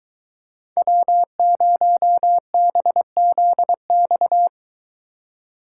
Morse code, heard as a keyed tone, about 22 wpm, 700 Hz. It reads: W0BZX